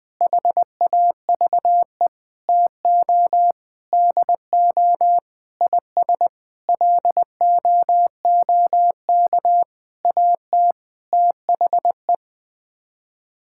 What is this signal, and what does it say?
Morse code, heard as a keyed tone, about 21 wpm, 700 Hz. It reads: HAVE TO DO IS LOOK AT THE
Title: Sets of 9 Words - 20wpm